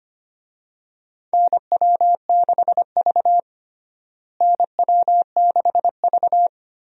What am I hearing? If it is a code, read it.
Morse code, heard as a keyed tone, about 25 wpm, 700 Hz. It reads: NW6V NW6V
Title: USA Call Signs - 4 Characters - Repeated 3x - 25wpm